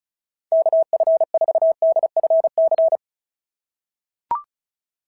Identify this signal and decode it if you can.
Morse code, heard as a keyed tone, about 36 wpm, 690 Hz. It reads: KF4DFC
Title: SOTA QSO Practice 35wpm